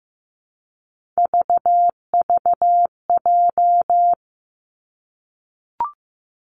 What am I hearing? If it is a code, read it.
Morse code, heard as a keyed tone, about 15 wpm, 700 Hz. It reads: VVJ
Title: Three Challenging Letter-Number Combinations 15wpm